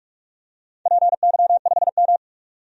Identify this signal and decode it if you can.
Morse code, heard as a keyed tone, about 41 wpm, 700 Hz. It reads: PY5M
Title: DXCC-100 Call Signs - 4 Characters - 45wpm